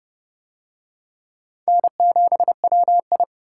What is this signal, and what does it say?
Morse code, heard as a keyed tone, about 29 wpm, 700 Hz. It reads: N7WI